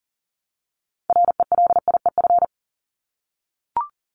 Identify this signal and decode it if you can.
Morse code, heard as a keyed tone, about 40 wpm, 700 Hz. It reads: RELIEF
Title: Top 1501-1750 Words 40wpm